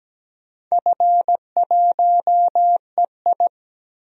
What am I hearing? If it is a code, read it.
Morse code, heard as a keyed tone, about 17 wpm, 700 Hz. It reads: F1EI
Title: DXCC-100 Call Signs - 4 Characters - 17wpm